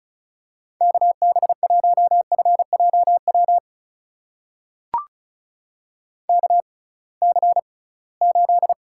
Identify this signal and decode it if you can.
Morse code, heard as a keyed tone, about 36 wpm, spaced as for 23 wpm, 690 Hz. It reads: KB1FJW KC8
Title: USA Call Signs - 6 Characters - 6x Character Spacing - 35wpm